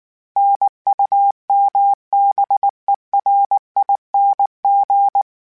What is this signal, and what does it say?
Morse code, heard as a keyed tone, about 19 wpm, 800 Hz. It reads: NUMBERING
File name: Special INGs / Special INGs-mp3s / Sngl Ing @18